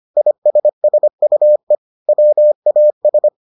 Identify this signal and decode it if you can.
Morse code, heard as a keyed tone, about 25 wpm, 600 Hz. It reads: ISSUE WAS